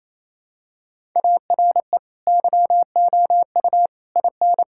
Morse code, heard as a keyed tone, about 28 wpm, 700 Hz. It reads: ARE YOU IN
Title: Sentences from Top 1000 Words - Review Part 5 - 28wpm